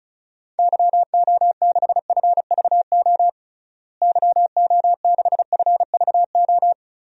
Morse code, heard as a keyed tone, about 36 wpm, 700 Hz. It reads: YO6FVO YO6FVO
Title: DXCC-100 Call Signs - 6 Characters - Repeated 3x - 35wpm